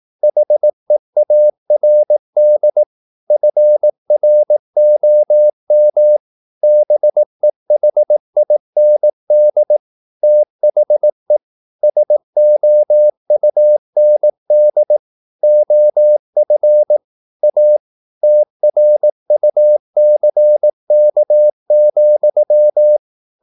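Morse code, heard as a keyed tone, about 18 wpm, 600 Hz. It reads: HEARD FROM BEHIND THE SOUND OF A TRUCK,
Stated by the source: Morse practice at 18 wpm, English text, from CW med Gnister.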